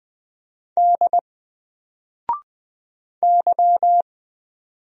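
Morse code, heard as a keyed tone, about 20 wpm, 700 Hz. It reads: D Y
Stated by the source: Single letter and number Morse drill at 20 wpm, from N4LCD.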